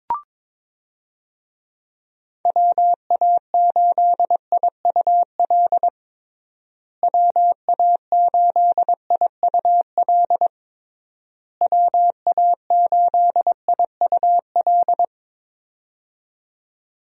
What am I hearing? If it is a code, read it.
Morse code, heard as a keyed tone, about 22 wpm, 700 Hz. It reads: WA8IUL WA8IUL WA8IUL
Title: USA Call Signs - 6 Characters - Repeated 3x - 22wpm